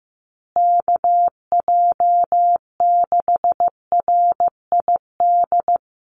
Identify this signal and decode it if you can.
Morse code, heard as a keyed tone, about 15 wpm, 700 Hz. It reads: KJ6RID
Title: USA Call Signs Repeated 3x 15wpm